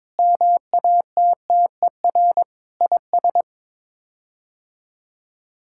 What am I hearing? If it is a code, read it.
Morse code, heard as a keyed tone, about 22 wpm, 700 Hz. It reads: MATTER IS
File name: practice / 22wpm / Sets of 6 Words - 22wpm